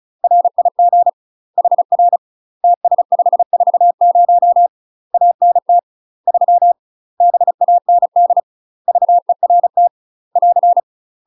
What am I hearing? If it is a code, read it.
Morse code, heard as a keyed tone, about 35 wpm, 700 Hz. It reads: RIG HR TS540 ANT 3 BAND VERT <AR>
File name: UK MASTER QSOs TRAINING / 35WPM / UK MASTER QSOs 0013